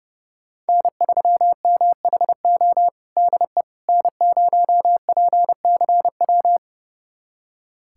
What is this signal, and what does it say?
Morse code, heard as a keyed tone, about 29 wpm, 700 Hz. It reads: N3MHO DE N0PCW
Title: USA Callsign DE Callsign - 5 Characters - 30wpm